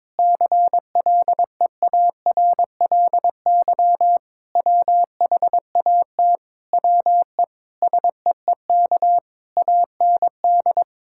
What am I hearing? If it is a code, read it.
Morse code, heard as a keyed tone, about 22 wpm, 700 Hz. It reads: CLEARLY WHAT WE SEEK AND
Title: Sentences from Top 1600 Words 22wpm